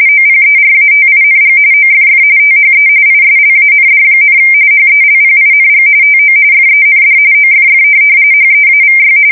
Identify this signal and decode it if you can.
RTTY, about 45.45 baud. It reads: OPERATING OFF THE ISLAND, HELPING TO FEND OFF HEAVY
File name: fsk170 / USS-Massachusetts